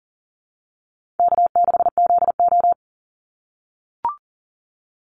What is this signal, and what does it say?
Morse code, heard as a keyed tone, about 40 wpm, 700 Hz. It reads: K6ZO
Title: Call Signs - 40wpm